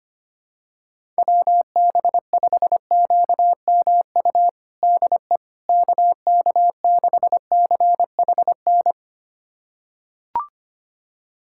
Morse code, heard as a keyed tone, about 25 wpm, 700 Hz. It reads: WB5QMU DE KK6CHN
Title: USA Callsign DE Callsign 25wpm